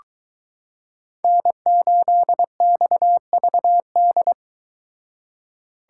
Morse code, heard as a keyed tone, about 22 wpm, 700 Hz. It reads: N8XVD